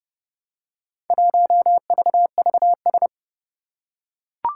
Morse code, heard as a keyed tone, about 30 wpm, 700 Hz. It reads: 1VVS
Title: Four Challenging Letter-Number Combinations 30wpm